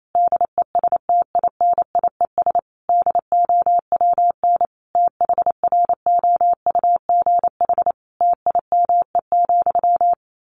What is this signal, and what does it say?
Morse code, heard as a keyed tone, about 28 wpm, 700 Hz. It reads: DESTINIES DOWN THROUGH TIME,
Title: A Sound Of Thunder 28wpm